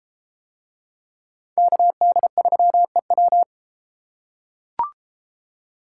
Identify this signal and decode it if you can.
Morse code, heard as a keyed tone, about 32 wpm, 700 Hz. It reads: KD3EW